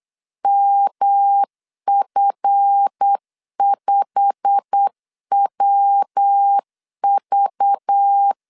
Morse code, heard as a keyed tone, about 8 wpm, 800 Hz. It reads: MF5WV
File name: LEZIONI / 8 YJ45 / 8 YJ45 8wpm 3214